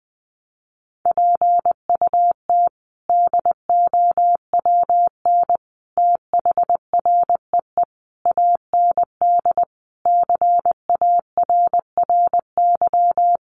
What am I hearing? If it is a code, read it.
Morse code, heard as a keyed tone, about 20 wpm, 700 Hz. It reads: PUT DOWN THREE AND CARRY
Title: Sentences from Top 400 Words 20wpm